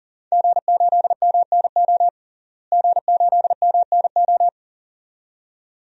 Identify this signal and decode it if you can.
Morse code, heard as a keyed tone, about 40 wpm, 700 Hz. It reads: G8MNO G8MNO
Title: DXCC-100 Call Signs - 5 Characters - Repeated 2x - 40wpm